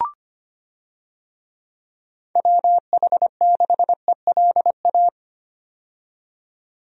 Morse code, heard as a keyed tone, about 25 wpm, 700 Hz. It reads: WH6ELA
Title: USA Call Signs - 6 Characters - 25wpm